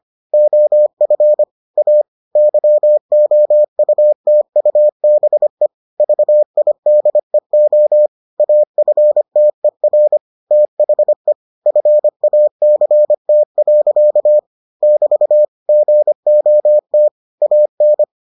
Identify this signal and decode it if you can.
Morse code, heard as a keyed tone, about 25 wpm, 600 Hz. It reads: OF A YOUTUBE VIDEO AFTER THE FACT. <BT> GOT AN